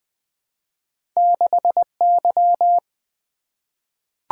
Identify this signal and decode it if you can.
Morse code, heard as a keyed tone, about 20 wpm, 700 Hz. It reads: 6Y
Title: Two Letter-Number Combinations - 20wpm